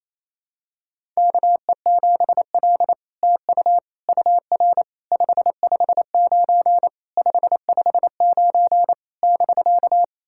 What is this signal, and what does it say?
Morse code, heard as a keyed tone, about 28 wpm, 700 Hz. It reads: KE7L TU UR 559 559 <BK>
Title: SOTA QSO Practice 28wpm